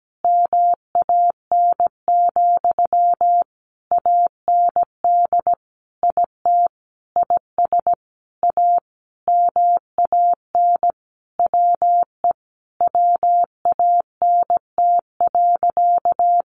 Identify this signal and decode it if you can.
Morse code, heard as a keyed tone, about 17 wpm, 700 Hz. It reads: MAN, AND IT IS A MAN WE WANT.
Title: Sentences from Top 200 Words 17wpm